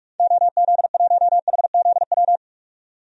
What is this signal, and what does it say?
Morse code, heard as a keyed tone, about 47 wpm, 700 Hz. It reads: OZ1HZW
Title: DXCC-100 Call Signs - 6 Characters - 45wpm